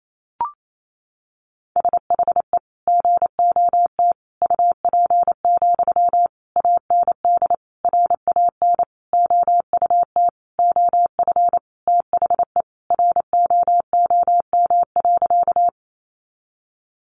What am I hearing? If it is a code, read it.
Morse code, heard as a keyed tone, about 28 wpm, 700 Hz. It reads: SHE GOT UP, AND RAN OUT OF THE ROOM.
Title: Sentences from Top 500 Words 28wpm